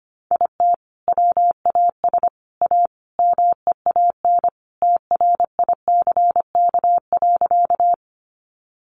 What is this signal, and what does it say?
Morse code, heard as a keyed tone, about 25 wpm, 700 Hz. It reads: IT WAS A MEAN TRICK.
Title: Sentences from Top 1500 Words 25wpm